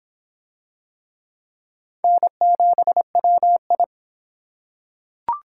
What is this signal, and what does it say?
Morse code, heard as a keyed tone, about 23 wpm, 700 Hz. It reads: N7WI